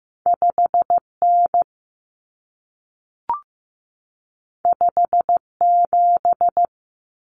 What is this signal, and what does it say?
Morse code, heard as a keyed tone, about 15 wpm, 700 Hz. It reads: 5N 57
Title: Common Cut Numbers - 2-Digit 15wpm